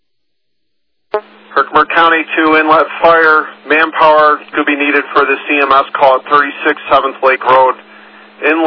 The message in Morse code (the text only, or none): none